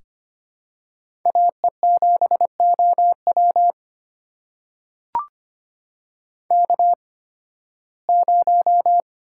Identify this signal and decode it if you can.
Morse code, heard as a keyed tone, about 25 wpm, spaced as for 16 wpm, 700 Hz. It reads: AE7OW K0
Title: USA Call Signs - 5 Characters - 8x Character Spacing - 25wpm